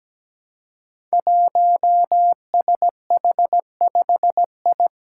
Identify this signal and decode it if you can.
Morse code, heard as a keyed tone, about 17 wpm, 700 Hz. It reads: 1SH5I
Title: Five Challenging Letter-Number Combinations 17wpm